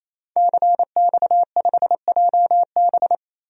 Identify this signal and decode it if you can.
Morse code, heard as a keyed tone, about 28 wpm, 700 Hz. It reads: CX5JB
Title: DXCC-100 Call Signs - 5 Characters - Repeated 3x - 28wpm